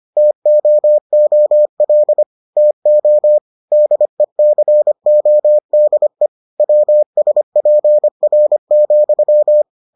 Morse code, heard as a keyed tone, about 25 wpm, 600 Hz. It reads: TOOL TO DECODE WSPR,